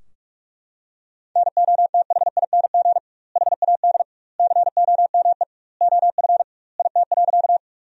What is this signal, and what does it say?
Morse code, heard as a keyed tone, about 41 wpm, 700 Hz. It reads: NOTHING HAD COME OF IT.
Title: 2x Word-Spacing - Sentences from Top 500 Words - 45wpm